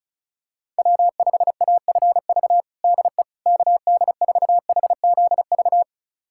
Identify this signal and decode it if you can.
Morse code, heard as a keyed tone, about 36 wpm, 700 Hz. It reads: W5AFV DE KD4HZV
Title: USA Callsign DE Callsign 35wpm